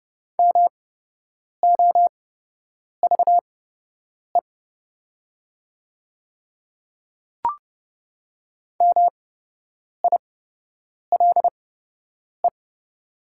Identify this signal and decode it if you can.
Morse code, heard as a keyed tone, about 30 wpm, spaced as for 8 wpm, 700 Hz. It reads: MOVE MILE
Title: 8x Character Spacing - Top 300 Words - 30wpm